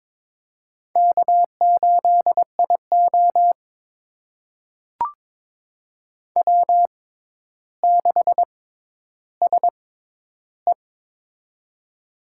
Morse code, heard as a keyed tone, about 22 wpm, spaced as for 11 wpm, 700 Hz. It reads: K8IO W6SE
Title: USA Call Signs - 4 Characters - 6x Character Spacing - 22wpm